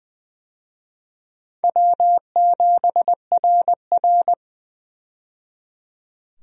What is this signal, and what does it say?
Morse code, heard as a keyed tone, about 20 wpm, 700 Hz. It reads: W7RR
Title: USA Call Signs - 4 Characters - 20wpm